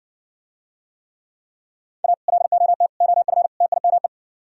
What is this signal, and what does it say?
Morse code, heard as a keyed tone, about 60 wpm, 700 Hz. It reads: A LOT OF TIME